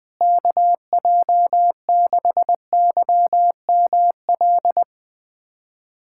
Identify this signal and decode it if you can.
Morse code, heard as a keyed tone, about 21 wpm, 700 Hz. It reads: KJ6YML
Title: USA Callsign DE Callsign - 6 Characters - 20wpm